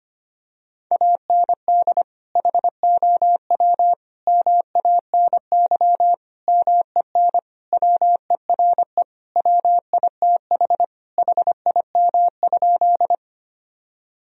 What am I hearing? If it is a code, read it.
Morse code, heard as a keyed tone, about 25 wpm, 700 Hz. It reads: AND HOW MANY MEN WERE WITH HIM?
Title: Sentences from Top 1000 Words - Review Part 1 - 25wpm